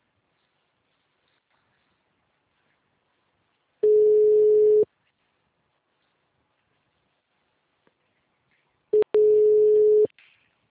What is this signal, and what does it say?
Ringback tone (Europe, 425 Hz).
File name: out- -2040- - - .96